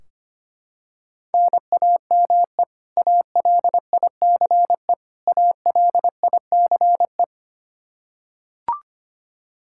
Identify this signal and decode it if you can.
Morse code, heard as a keyed tone, about 25 wpm, 700 Hz. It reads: NAME ALICE ALICE
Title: Practice QSO - Exchange 1 - Beginning - 25wpm